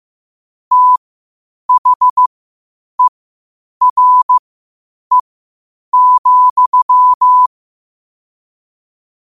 Morse code, heard as a keyed tone, about 15 wpm, spaced as for 8 wpm, 1000 Hz. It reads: THERE,
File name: qotdfiles / 8wpm